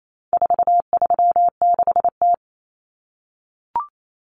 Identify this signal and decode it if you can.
Morse code, heard as a keyed tone, about 28 wpm, 700 Hz. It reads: 436T
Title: Common Cut Numbers - 4-Digit 28wpm